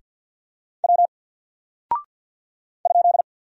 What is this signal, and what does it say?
Morse code, heard as a keyed tone, about 48 wpm, 700 Hz. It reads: W ?
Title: Single Character (Letter:Number:Punctuation:Prosign) 50wpm